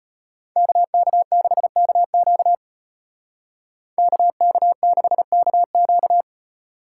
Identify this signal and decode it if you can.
Morse code, heard as a keyed tone, about 35 wpm, 700 Hz. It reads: KK6KQ KK6KQ